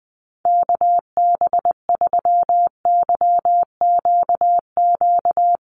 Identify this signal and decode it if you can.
Morse code, heard as a keyed tone, about 20 wpm, 700 Hz. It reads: KB3YQQ
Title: USA Call Signs Repeated 3x 20wpm